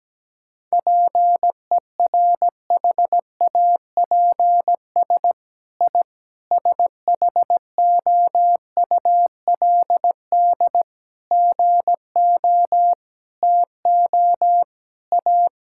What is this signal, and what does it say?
Morse code, heard as a keyed tone, about 17 wpm, 700 Hz. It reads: PERHAPS I SHOULD GO TO A
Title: Sentences from Top 1000 Words - Review Part 5 - 17wpm